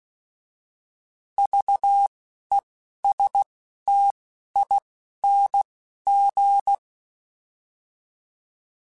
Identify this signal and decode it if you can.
Morse code, heard as a keyed tone, about 16 wpm, spaced as for 11 wpm, 800 Hz. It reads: VESTING